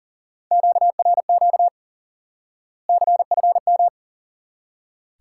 Morse code, heard as a keyed tone, about 40 wpm, 700 Hz. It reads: QRQ CFM
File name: course / instant-qso-element-course-3 / IQE Course 3 - Lesson 89 - YAGI - Two Elements - 40wpm